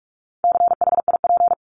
Morse code, heard as a keyed tone, about 44 wpm, 700 Hz. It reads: CHIP
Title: Top 1751-2000 Words 45wpm